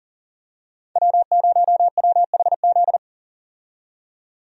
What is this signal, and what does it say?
Morse code, heard as a keyed tone, about 38 wpm, 700 Hz. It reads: W0WHZ
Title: USA Call Signs - 5 Characters - 40wpm